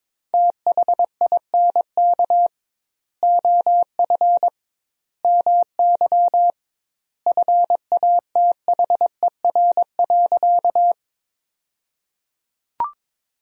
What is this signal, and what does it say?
Morse code, heard as a keyed tone, about 21 wpm, 700 Hz. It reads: THINK OF MY FATHER.
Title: 2x Word-Spacing - Sentences from Top 300 Words - 22wpm